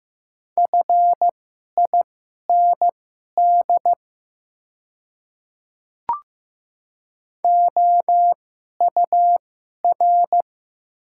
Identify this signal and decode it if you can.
Morse code, heard as a keyed tone, about 15 wpm, spaced as for 11 wpm, 700 Hz. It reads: FIND OUR
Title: 2x Character Spacing - Top 300 Words - 15wpm